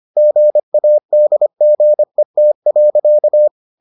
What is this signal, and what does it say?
Morse code, heard as a keyed tone, about 25 wpm, 600 Hz. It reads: GADGET.